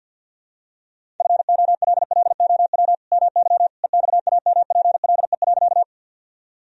Morse code, heard as a keyed tone, about 52 wpm, 700 Hz. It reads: FOLLOW MY EXAMPLE.
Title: Sentences from Top 1000 Words - Review Part 1 - 50wpm